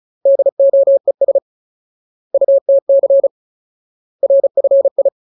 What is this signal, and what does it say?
Morse code, heard as a keyed tone, about 34 wpm, 550 Hz. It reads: DOES UTC RFI
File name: Ultimate Word List / 35WPM Ultimate